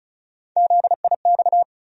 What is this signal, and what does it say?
Morse code, heard as a keyed tone, about 36 wpm, 700 Hz. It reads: ZIX
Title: USA Call Sign Postfixes - 3 Letters 35wpm